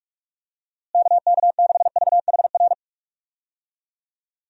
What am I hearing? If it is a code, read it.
Morse code, heard as a keyed tone, about 43 wpm, 700 Hz. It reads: KK6VHR
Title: USA Call Signs - 6 Characters - Repeated 3x - 45wpm